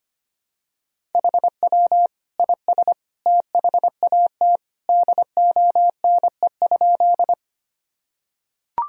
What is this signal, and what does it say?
Morse code, heard as a keyed tone, about 25 wpm, 700 Hz. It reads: HW IS THAT DONE?
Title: Sentences from Top 500 Words and Abbreviations 25wpm